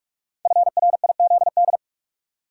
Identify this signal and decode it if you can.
Morse code, heard as a keyed tone, about 44 wpm, 700 Hz. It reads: FRIZD
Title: Code Groups - 5 Letter:Numbers - 45wpm